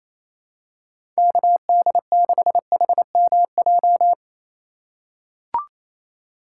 Morse code, heard as a keyed tone, about 28 wpm, 700 Hz. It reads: KD6HMJ
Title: USA Call Signs - 6 Characters - 4x Character Spacing - 28wpm